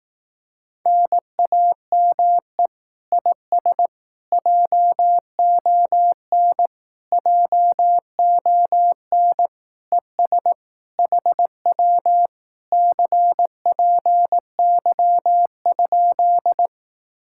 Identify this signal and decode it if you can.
Morse code, heard as a keyed tone, about 18 wpm, 700 Hz. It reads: NAME IS JON JON ES HW CPY?
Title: Speed-Racing Common QSO Phrases Encore - 15wpm